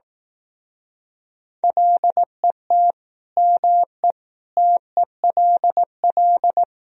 Morse code, heard as a keyed tone, about 18 wpm, 700 Hz. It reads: LET ME TELL
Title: Speed-Racing - Sets of 4 Words 15wpm